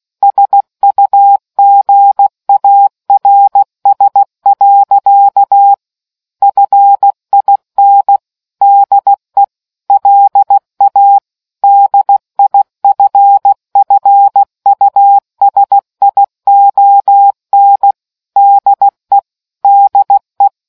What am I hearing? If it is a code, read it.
Morse code, heard as a keyed tone, about 16 wpm, 800 Hz. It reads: SUGARS. FIN DE LA DIFFUSION DE DE